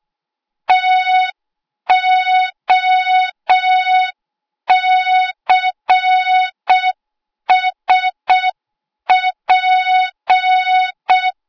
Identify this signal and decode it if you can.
Morse code, heard as a keyed tone, about 6 wpm, 3700 Hz. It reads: TOCSP
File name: Morze-HA2MN / hang / 08KXC HA2MN